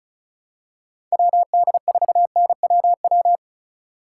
Morse code, heard as a keyed tone, about 36 wpm, 700 Hz. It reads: WD4NWW
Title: USA Call Signs - 6 Characters - 3x Character Spacing - 35wpm